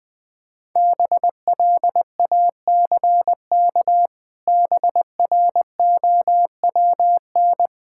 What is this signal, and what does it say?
Morse code, heard as a keyed tone, about 21 wpm, 700 Hz. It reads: BLACK BROWN
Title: Pangram Challenge v2 - 20wpm